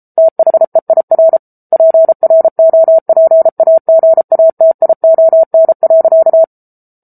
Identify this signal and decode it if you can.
Morse code, heard as a keyed tone, about 34 wpm, 650 Hz. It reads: THEIR PROPAGATION.
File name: Book Two / 07-Chapter 7 - 33 WPM